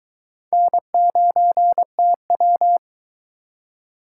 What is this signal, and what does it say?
Morse code, heard as a keyed tone, about 22 wpm, 700 Hz. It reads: N9TW